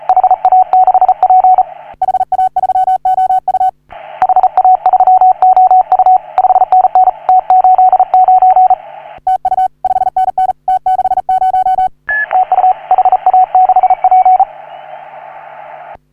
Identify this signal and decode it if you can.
Morse code, heard as a keyed tone, about 39 wpm, 750 Hz. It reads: HA6P HA3OU HA3OU 5NN T89 TU 5NN T60 TU HA6P
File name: CW / HA3OU RHOBCW REC